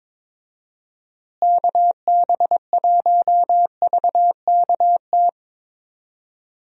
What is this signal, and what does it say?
Morse code, heard as a keyed tone, about 22 wpm, 700 Hz. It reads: KB1VKT
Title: USA Call Signs - 6 Characters - Repeated 3x - 22wpm